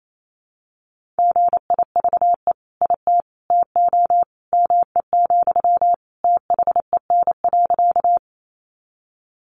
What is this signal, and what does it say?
Morse code, heard as a keyed tone, about 28 wpm, 700 Hz. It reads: GIVE IT TO ME, THEN.
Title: Sentences from Top 200 Words 28wpm